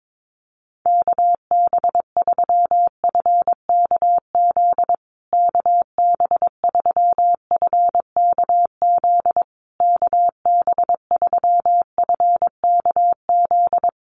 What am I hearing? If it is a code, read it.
Morse code, heard as a keyed tone, about 22 wpm, 700 Hz. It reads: KB3FKZ KB3FKZ KB3FKZ
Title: USA Call Signs Repeated 3x 22wpm